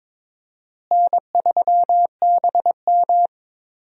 Morse code, heard as a keyed tone, about 22 wpm, 700 Hz. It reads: N3BM
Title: USA Call Signs - 4 Characters - Repeated 2x - 22wpm